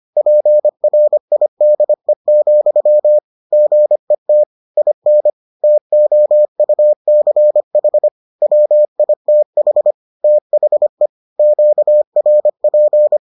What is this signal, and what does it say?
Morse code, heard as a keyed tone, about 25 wpm, 600 Hz. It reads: PRIDE, GET IN TOUCH WITH THE QRP